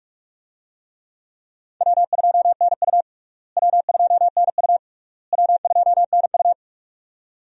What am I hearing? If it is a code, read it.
Morse code, heard as a keyed tone, about 46 wpm, 700 Hz. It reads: W2NU W2NU W2NU
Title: USA Call Signs - 4 Characters - Repeated 3x - 45wpm